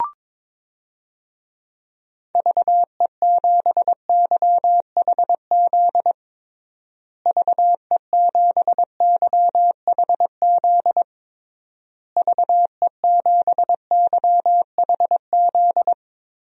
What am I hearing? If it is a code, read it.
Morse code, heard as a keyed tone, about 22 wpm, 700 Hz. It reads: VE7YHZ VE7YHZ VE7YHZ
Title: DXCC-100 Call Signs - 6 Characters - Repeated 3x - 22wpm